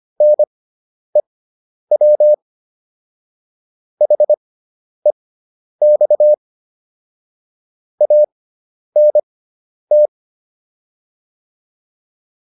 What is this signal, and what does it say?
Morse code, heard as a keyed tone, about 25 wpm, spaced as for 10 wpm, 600 Hz. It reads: NEW HEX ANT